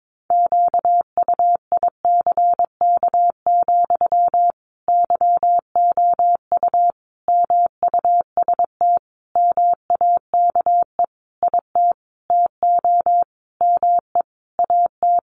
Morse code, heard as a keyed tone, about 22 wpm, 700 Hz. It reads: QUICK, YOU MUST MAKE IT TO ME AT
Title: Sentences from Top 500 Words 22wpm